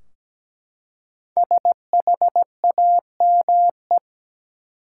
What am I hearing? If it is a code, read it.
Morse code, heard as a keyed tone, about 17 wpm, 700 Hz. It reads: SHAME
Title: Five Letter Words - 17wpm